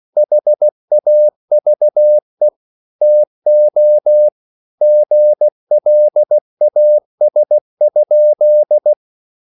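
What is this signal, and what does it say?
Morse code, heard as a keyed tone, about 16 wpm, 600 Hz. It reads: HAVE TO GLAS?
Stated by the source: Slow (QRS) Morse code at 16 wpm, from CW med Gnister.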